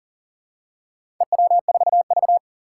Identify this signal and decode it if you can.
Morse code, heard as a keyed tone, about 39 wpm, 700 Hz. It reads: EW4V
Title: Four Challenging Letter-Number Combinations 40wpm